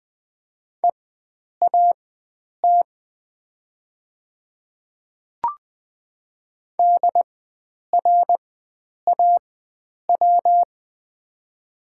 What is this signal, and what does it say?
Morse code, heard as a keyed tone, about 20 wpm, spaced as for 9 wpm, 700 Hz. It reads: EAT DRAW
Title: 4x Character Spacing - Top 300 Words - 20wpm